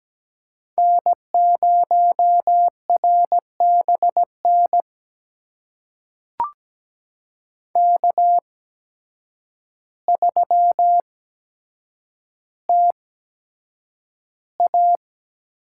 Morse code, heard as a keyed tone, about 17 wpm, spaced as for 8 wpm, 700 Hz. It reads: N0RBN K3TA
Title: USA Call Signs - 8x Character Spacing - 17wpm